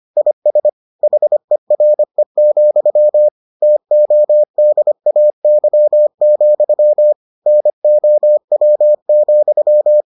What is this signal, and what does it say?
Morse code, heard as a keyed tone, about 25 wpm, 600 Hz. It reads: IS HERE, TODAY, NOW,